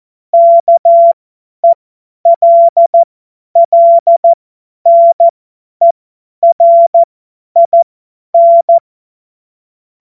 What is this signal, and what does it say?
Morse code, heard as a keyed tone, about 14 wpm, spaced as for 10 wpm, 680 Hz. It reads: KELLNERIN
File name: Morsekurs / Witze / Witz 06. April 2021, 50 F70